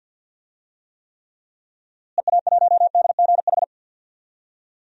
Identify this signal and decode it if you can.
Morse code, heard as a keyed tone, about 48 wpm, 700 Hz. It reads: EA1DGH
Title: DXCC-100 Call Signs - 6 Characters - Repeated 3x - 50wpm